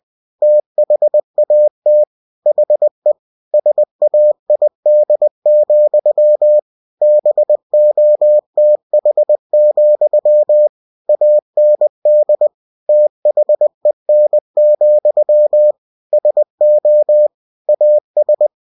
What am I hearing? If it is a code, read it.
Morse code, heard as a keyed tone, about 20 wpm, 600 Hz. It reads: THAT HE SAID, BOTH, AND THEN, SO AS
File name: cw-books / winnie-the-pooh / winnie-20wpm-page-0015